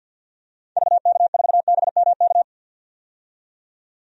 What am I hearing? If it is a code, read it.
Morse code, heard as a keyed tone, about 50 wpm, 700 Hz. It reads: VK4BMK
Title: DXCC-100 Call Signs - 6 Characters - 50wpm